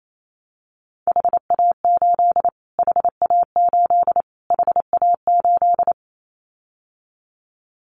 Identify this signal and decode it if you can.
Morse code, heard as a keyed tone, about 28 wpm, 700 Hz. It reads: HA8 HA8 HA8
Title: Callsign Prefixes for Top 25 DXCC - Repeated 3x - 28wpm